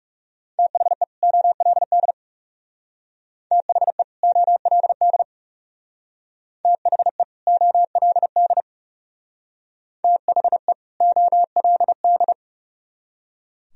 Morse code, heard as a keyed tone, about 37 wpm, 700 Hz. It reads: THE OLD THE OLD THE OLD THE OLD
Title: Speed-Racing - Sets of 2 Words 30wpm